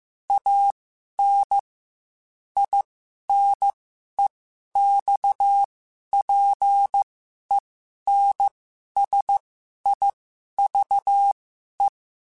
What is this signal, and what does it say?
Morse code, heard as a keyed tone, about 15 wpm, spaced as for 11 wpm, 800 Hz. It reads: AN INEXPENSIVE
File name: Quotes / Quotes-mp3s / A Smile @13